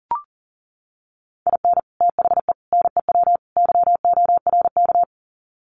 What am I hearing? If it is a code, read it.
Morse code, heard as a keyed tone, about 40 wpm, 700 Hz. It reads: IN THE NEW YORK
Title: Sets of 4 Words 40wpm